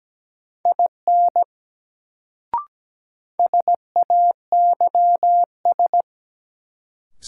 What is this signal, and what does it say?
Morse code, heard as a keyed tone, about 17 wpm, 700 Hz. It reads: IN SAYS
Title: Sets of 2 Words - New Word Drill - 17wpm